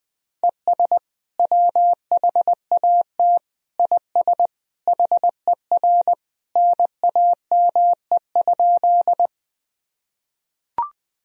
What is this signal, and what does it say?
Morse code, heard as a keyed tone, about 20 wpm, 700 Hz. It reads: ES WHAT IS HER NAME?
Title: Sentences from Top 200 Words and Abbreviations 20wpm